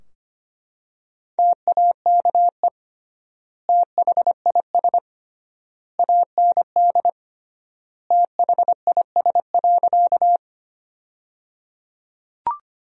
Morse code, heard as a keyed tone, about 25 wpm, 700 Hz. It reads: TAKE THIS AND THIS.
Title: 3x Word-Spacing - Sentences from Top 200 Words - 25wpm